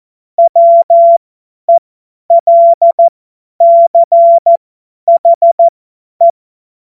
Morse code, heard as a keyed tone, about 14 wpm, spaced as for 10 wpm, 680 Hz. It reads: WELCHE
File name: Morsekurs / Witze / Witz 21. September 2021, 50 F70